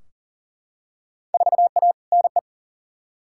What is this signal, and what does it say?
Morse code, heard as a keyed tone, about 40 wpm, 700 Hz. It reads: 4A NE
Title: Field Day Exchanges - 40wpm